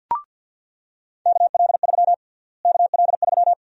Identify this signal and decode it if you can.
Morse code, heard as a keyed tone, about 49 wpm, 700 Hz. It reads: KL3 KL3
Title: USA Call Sign Prefixes - Repeated 2x - 50wpm